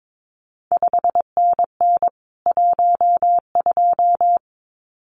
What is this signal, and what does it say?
Morse code, heard as a keyed tone, about 22 wpm, 700 Hz. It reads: 5NN 12
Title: CQ WW DX Contest Exchange Encore - 22wpm